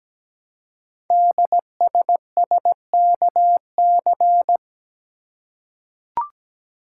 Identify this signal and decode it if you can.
Morse code, heard as a keyed tone, about 17 wpm, 700 Hz. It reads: DSSKC
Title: Code Groups - 5 Letters - 17wpm